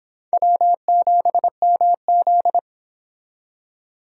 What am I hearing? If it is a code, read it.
Morse code, heard as a keyed tone, about 26 wpm, 700 Hz. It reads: W7MZ